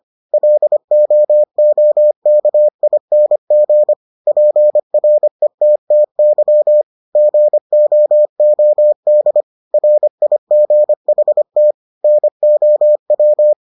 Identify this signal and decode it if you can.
Morse code, heard as a keyed tone, about 25 wpm, 600 Hz. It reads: LOOKING PRETTY GOOD RIGHT NOW